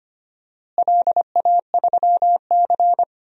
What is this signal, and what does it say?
Morse code, heard as a keyed tone, about 25 wpm, 700 Hz. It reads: LA3C
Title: DXCC-100 Call Signs - 4 Characters - Repeated 3x - 25wpm